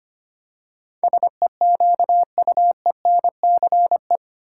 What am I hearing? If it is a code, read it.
Morse code, heard as a keyed tone, about 25 wpm, 700 Hz. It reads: SEQUENCE
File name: Eight Letter Words - 25wpm